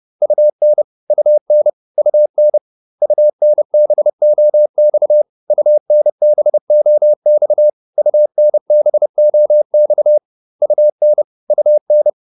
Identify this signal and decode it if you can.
Morse code, heard as a keyed tone, about 30 wpm, 600 Hz. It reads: UN UN UN UNBOX UNBOX UNBOX UN UN